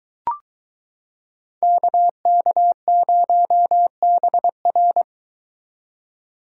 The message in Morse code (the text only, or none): KK0BR